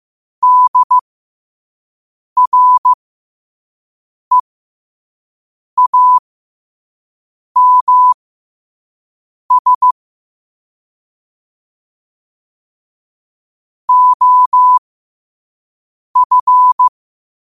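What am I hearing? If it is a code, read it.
Morse code, heard as a keyed tone, about 15 wpm, spaced as for 5 wpm, 1000 Hz. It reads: DREAMS OF
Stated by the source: Morse code sent at 5 wpm.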